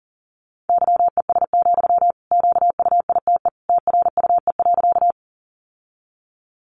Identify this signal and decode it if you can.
Morse code, heard as a keyed tone, about 37 wpm, 700 Hz. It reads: YES, QUITE TRUE.
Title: Sentences from Top 700 Words 40wpm